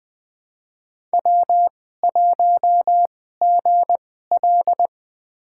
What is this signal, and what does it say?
Morse code, heard as a keyed tone, about 20 wpm, spaced as for 14 wpm, 700 Hz. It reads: W1GL
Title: USA Call Signs - 4 Characters - 2x Character Spacing - 20wpm